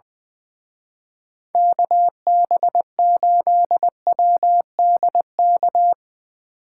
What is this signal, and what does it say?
Morse code, heard as a keyed tone, about 20 wpm, 700 Hz. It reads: KB8WDK
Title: Speed Racing - USA 2x3 Call Signs 20wpm